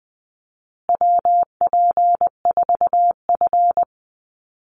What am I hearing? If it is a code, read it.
Morse code, heard as a keyed tone, about 20 wpm, 700 Hz. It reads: WP4F
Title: Call Signs Repeated 2x - 20wpm